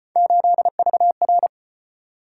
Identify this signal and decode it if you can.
Morse code, heard as a keyed tone, about 34 wpm, 700 Hz. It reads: 8VR